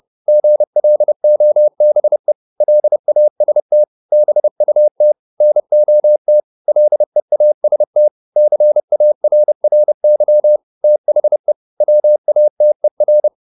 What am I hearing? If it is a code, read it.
Morse code, heard as a keyed tone, about 30 wpm, 600 Hz. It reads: GLOBE LAST BUT NOT LEAST CARRY THE WATER